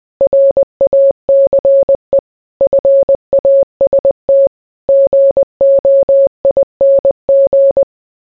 Morse code, heard as a keyed tone, about 20 wpm, 550 Hz. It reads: RACE FAST GOING